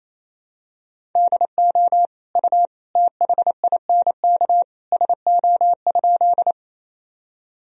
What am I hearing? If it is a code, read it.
Morse code, heard as a keyed tone, about 28 wpm, 700 Hz. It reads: DO U THINK SO?
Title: Sentences from Top 200 Words and Abbreviations 28wpm